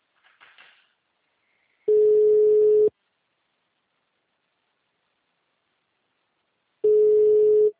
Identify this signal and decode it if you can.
Ringback tone (Europe, 425 Hz).
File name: out- -2005- - - .877